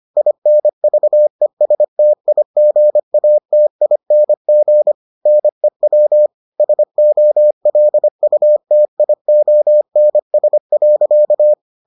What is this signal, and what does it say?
Morse code, heard as a keyed tone, about 25 wpm, 600 Hz. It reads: INVESTIGATING NEW SOLUTIONS.